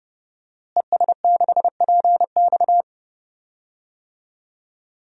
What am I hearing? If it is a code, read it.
Morse code, heard as a keyed tone, about 30 wpm, 700 Hz. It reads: ES6PX
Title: DXCC-100 Call Signs - 5 Characters - Repeated 3x - 30wpm